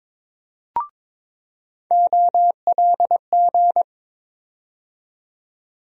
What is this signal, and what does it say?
Morse code, heard as a keyed tone, about 22 wpm, 700 Hz. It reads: OLG
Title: USA Call Sign Postfixes - 3 Letters 22wpm